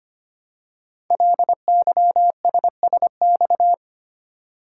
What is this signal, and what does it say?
Morse code, heard as a keyed tone, about 25 wpm, 700 Hz. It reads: LYSSX
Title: Code Groups - 5 Letters - 25wpm